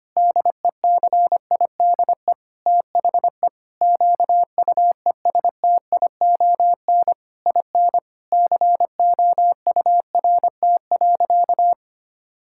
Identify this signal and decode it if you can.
Morse code, heard as a keyed tone, about 25 wpm, 700 Hz. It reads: DECIDE THE QUESTION IN COURT.
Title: Sentences from Top 1600 Words 25wpm